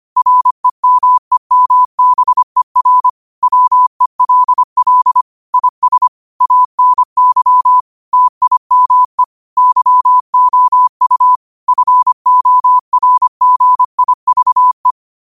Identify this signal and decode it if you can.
Morse code, heard as a keyed tone, about 25 wpm, 1000 Hz. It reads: REMEMBER WELL IS ANY TIME YOU FORGIVE
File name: qotdfiles / 25wpm